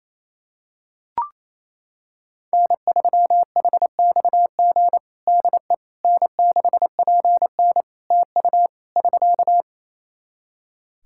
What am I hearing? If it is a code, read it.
Morse code, heard as a keyed tone, about 27 wpm, 700 Hz. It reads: N3HXG DE N6PN TU <SK>
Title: K1USN SST QSO Practice 28wpm